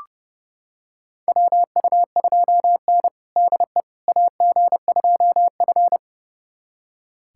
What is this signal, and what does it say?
Morse code, heard as a keyed tone, about 30 wpm, 700 Hz. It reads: WU2N DE AG2F
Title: USA Callsign DE Callsign - 4 Characters - 30wpm